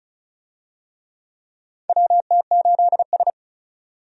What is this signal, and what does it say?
Morse code, heard as a keyed tone, about 36 wpm, 700 Hz. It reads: WT8S
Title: USA Call Signs - 4 Characters - 35wpm